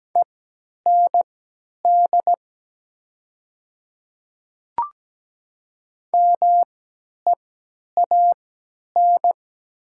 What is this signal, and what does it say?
Morse code, heard as a keyed tone, about 17 wpm, spaced as for 10 wpm, 700 Hz. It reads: END MEAN
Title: 3x Character Spacing - Top 200 Words - 17wpm